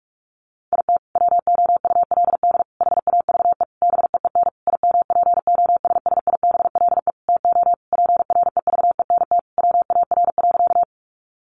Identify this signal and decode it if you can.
Morse code, heard as a keyed tone, about 44 wpm, 700 Hz. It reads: IT WOULD HAVE BEEN IMPOSSIBLE TO PREVENT WAR.
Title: Sentences from Top 1100 Words 45wpm